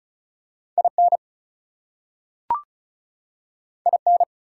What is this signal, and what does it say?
Morse code, heard as a keyed tone, about 36 wpm, 700 Hz. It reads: IN IN
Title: Two Letter Words - 35wpm